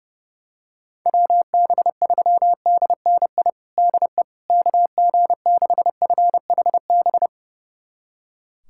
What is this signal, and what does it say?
Morse code, heard as a keyed tone, about 30 wpm, 700 Hz. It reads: WB3DNI DE KG6FHB
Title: USA Callsign DE Callsign - 6 Characters - 30wpm